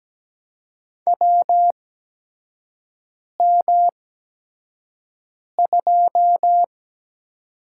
Morse code, heard as a keyed tone, about 17 wpm, spaced as for 5 wpm, 700 Hz. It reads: WM2
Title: USA Call Signs - 8x Character Spacing - 17wpm